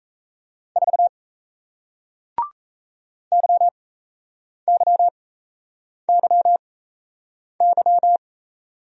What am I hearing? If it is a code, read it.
Morse code, heard as a keyed tone, about 38 wpm, spaced as for 8 wpm, 700 Hz. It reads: 4 YYYY